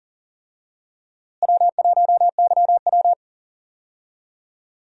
Morse code, heard as a keyed tone, about 40 wpm, 700 Hz. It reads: W1YW
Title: USA Call Signs - 4 Characters - Repeated 2x - 40wpm